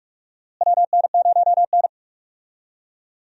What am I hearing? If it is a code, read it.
Morse code, heard as a keyed tone, about 45 wpm, 700 Hz. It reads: WN0N
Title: USA Call Signs - 4 Characters - Repeated 2x - 45wpm